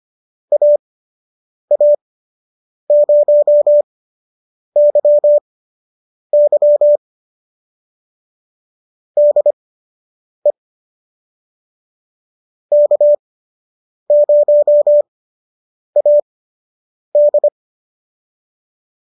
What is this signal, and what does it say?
Morse code, heard as a keyed tone, about 25 wpm, spaced as for 8 wpm, 570 Hz. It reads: AA0YY DE K0AD